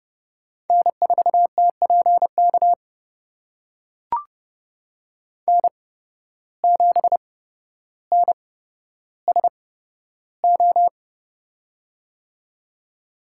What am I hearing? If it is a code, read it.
Morse code, heard as a keyed tone, about 31 wpm, spaced as for 13 wpm, 700 Hz. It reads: N4TPK N7NSO
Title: USA Call Signs - 5 Characters - 8x Character Spacing - 30wpm